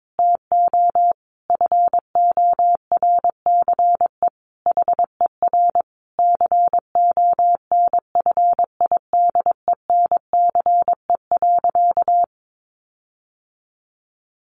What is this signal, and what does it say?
Morse code, heard as a keyed tone, about 22 wpm, 700 Hz. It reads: TO FORCE HER CONFIDENCE.
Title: Sentences from Top 1500 Words 22wpm